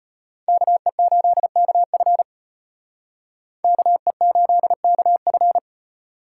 Morse code, heard as a keyed tone, about 34 wpm, 700 Hz. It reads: KE8KF KE8KF